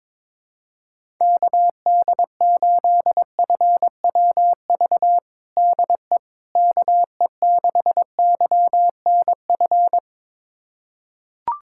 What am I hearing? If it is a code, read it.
Morse code, heard as a keyed tone, about 22 wpm, 700 Hz. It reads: KD8FWV DE KE6YNF
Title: USA Callsign DE Callsign 22wpm